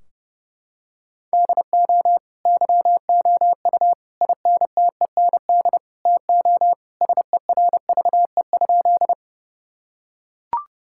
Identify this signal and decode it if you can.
Morse code, heard as a keyed tone, about 30 wpm, 700 Hz. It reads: DO YOU INTEND TO SERVE?
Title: Sentences from Top 1600 Words 30wpm